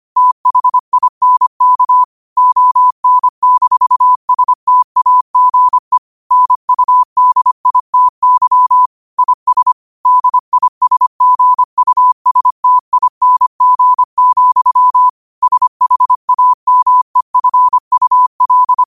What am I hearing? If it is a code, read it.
Morse code, heard as a keyed tone, about 25 wpm, 1000 Hz. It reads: THINK ON-STAGE NUDITY IS DISGUSTING, SHAMEFUL
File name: qotdfiles / 25wpm